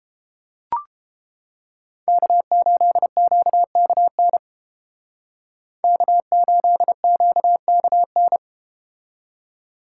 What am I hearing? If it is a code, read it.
Morse code, heard as a keyed tone, about 31 wpm, 700 Hz. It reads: K8QKN K8QKN